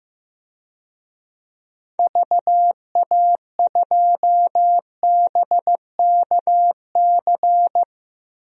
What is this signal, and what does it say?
Morse code, heard as a keyed tone, about 15 wpm, 700 Hz. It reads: VA2BKC
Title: DXCC-100 Call Signs - 6 Characters - 15wpm